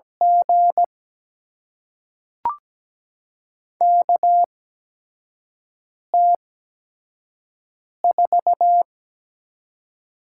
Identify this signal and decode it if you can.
Morse code, heard as a keyed tone, about 17 wpm, spaced as for 5 wpm, 700 Hz. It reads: G KT4
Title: USA Call Signs - 5 Characters - 8x Character Spacing - 17wpm